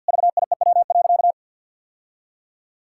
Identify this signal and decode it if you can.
Morse code, heard as a keyed tone, about 50 wpm, 700 Hz. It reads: VIEW.
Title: Sentences from Top 1000 Words - Review Part 6 - 50wpm